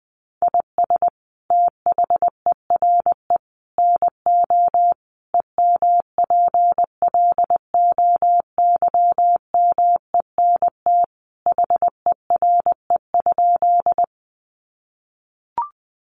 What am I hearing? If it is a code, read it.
Morse code, heard as a keyed tone, about 20 wpm, 700 Hz. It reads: IS THERE NO EMPLOYMENT HERE?
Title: Sentences from Top 1500 Words 20wpm